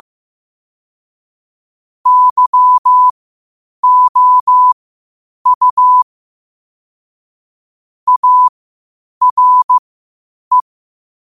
Morse code, heard as a keyed tone, about 15 wpm, spaced as for 8 wpm, 1000 Hz. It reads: YOU ARE
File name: qotdfiles / 8wpm